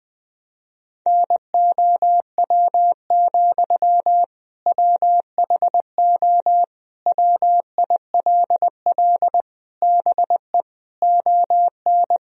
Morse code, heard as a keyed tone, about 20 wpm, 700 Hz. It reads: NOW, WHO WILL BE ON
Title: Sentences from Top 1000 Words - Review Part 1 - 20wpm